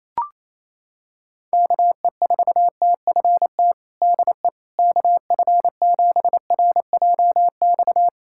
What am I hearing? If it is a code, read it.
Morse code, heard as a keyed tone, about 28 wpm, 700 Hz. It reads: KE4TFT DE KF7RJX
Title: USA Callsign DE Callsign - 6 Characters - 28wpm